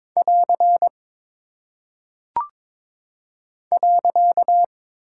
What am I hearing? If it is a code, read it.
Morse code, heard as a keyed tone, about 22 wpm, 700 Hz. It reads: <AR> .
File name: practice / 22wpm / Single Punctuation-Prosign - 22wpm